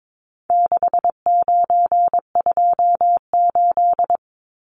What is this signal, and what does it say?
Morse code, heard as a keyed tone, about 22 wpm, 700 Hz. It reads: 6928
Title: Numbers - 4-Digit 22wpm